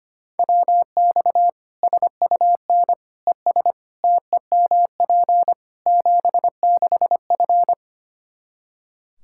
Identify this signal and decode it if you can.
Morse code, heard as a keyed tone, about 25 wpm, 700 Hz. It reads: WX SUN ES TEMP 76F
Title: K1USN SST QSO Practice 25wpm